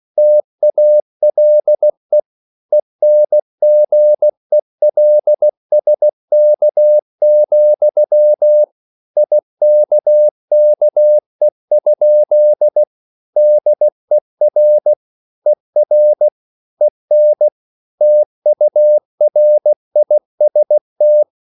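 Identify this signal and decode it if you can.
Morse code, heard as a keyed tone, about 16 wpm, 600 Hz. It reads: TALE ENGELSK, IKKE? DER ER EN TURIST
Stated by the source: Slow (QRS) Morse code at 16 wpm, from CW med Gnister.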